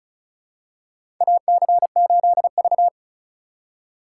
Morse code, heard as a keyed tone, about 34 wpm, 700 Hz. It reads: AC8V
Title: USA Call Signs - 4 Characters - 3x Character Spacing - 35wpm